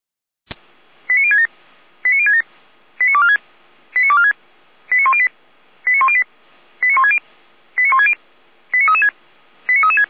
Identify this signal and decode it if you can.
Five-tone selective call (ZVEI), heard as quick runs of five tones, tones of about 70 ms. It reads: 89067 89067 89246 89246 89108 89108 89160 89160 89336 89336
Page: audio.f85360.feuerwehr-orlamuende.org